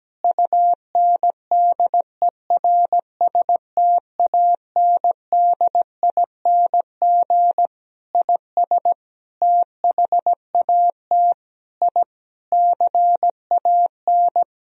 Morse code, heard as a keyed tone, about 17 wpm, 700 Hz. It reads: UNDERSTANDING IS THAT I CAN
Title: Sentences from Top 1000 Words - Review Part 3 - 17wpm